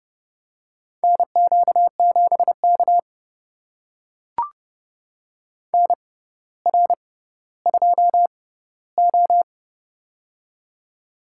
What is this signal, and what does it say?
Morse code, heard as a keyed tone, about 32 wpm, spaced as for 16 wpm, 700 Hz. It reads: NQ7K NR2O
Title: USA Call Signs - 4 Characters - 6x Character Spacing - 30wpm